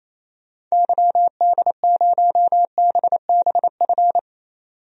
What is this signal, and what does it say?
Morse code, heard as a keyed tone, about 27 wpm, 700 Hz. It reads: YD0BBF
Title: DXCC-100 Call Signs - 6 Characters - 28wpm